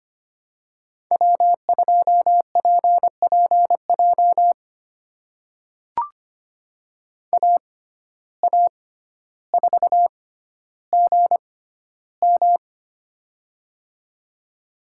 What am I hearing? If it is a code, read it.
Morse code, heard as a keyed tone, about 25 wpm, spaced as for 13 wpm, 700 Hz. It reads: W2PPJ AA4GM
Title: USA Call Signs - 5 Characters - 6x Character Spacing - 25wpm